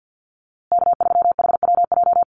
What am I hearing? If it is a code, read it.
Morse code, heard as a keyed tone, about 50 wpm, 700 Hz. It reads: K3HWJ
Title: USA Call Signs Repeated 3x 50wpm